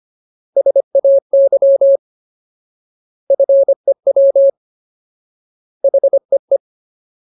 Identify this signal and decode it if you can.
Morse code, heard as a keyed tone, about 25 wpm, 550 Hz. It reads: SAY FEW HEE
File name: Ultimate Word List / 25WPM Ultimate